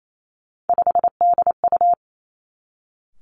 Morse code, heard as a keyed tone, about 28 wpm, 700 Hz. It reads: 5DU
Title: Three Challenging Letter-Number Combinations 28wpm